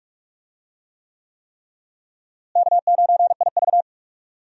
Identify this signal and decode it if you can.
Morse code, heard as a keyed tone, about 45 wpm, 700 Hz. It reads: K9IV
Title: USA Call Signs - 4 Characters - Repeated 3x - 45wpm